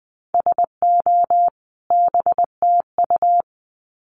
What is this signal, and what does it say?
Morse code, heard as a keyed tone, about 20 wpm, 700 Hz. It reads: SO BTU
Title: Common QSO Phrases 20wpm